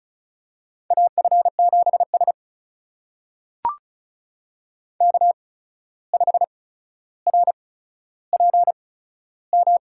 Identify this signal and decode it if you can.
Morse code, heard as a keyed tone, about 36 wpm, spaced as for 14 wpm, 700 Hz. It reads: AF7S K5RPM
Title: USA Call Signs - 8x Character Spacing - 35wpm